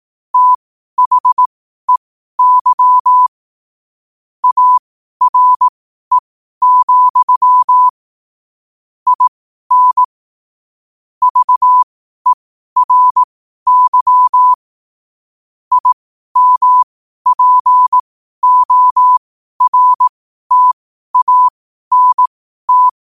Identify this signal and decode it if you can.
Morse code, heard as a keyed tone, about 18 wpm, spaced as for 12 wpm, 1000 Hz. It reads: THEY ARE, IN VERY IMPORTANT